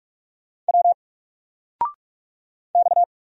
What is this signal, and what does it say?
Morse code, heard as a keyed tone, about 44 wpm, 700 Hz. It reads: W X
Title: Single Character (Letter:Number:Punctuation:Prosign) 45wpm